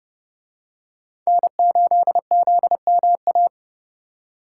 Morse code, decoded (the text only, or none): N8ZMA